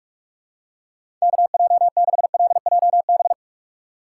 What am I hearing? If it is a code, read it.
Morse code, heard as a keyed tone, about 46 wpm, 700 Hz. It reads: KJ6LJB
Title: USA Call Signs - 2x Character Spacing - 45wpm